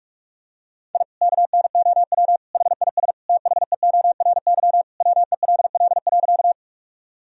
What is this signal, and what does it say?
Morse code, heard as a keyed tone, about 44 wpm, 700 Hz. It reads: I KNOW HIS THEORY WELL.
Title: Sentences from Top 1000 Words - Review Part 3 - 45wpm